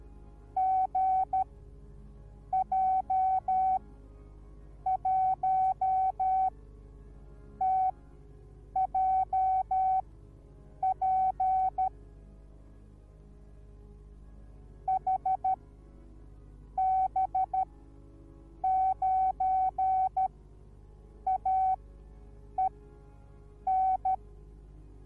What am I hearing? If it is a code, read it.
Morse code, heard as a keyed tone, about 13 wpm, spaced as for 6 wpm, 760 Hz. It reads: GJ1TJP HB9AEN